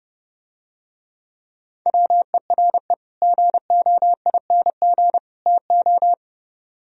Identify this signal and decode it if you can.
Morse code, heard as a keyed tone, about 30 wpm, 700 Hz. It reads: WERE GOING TO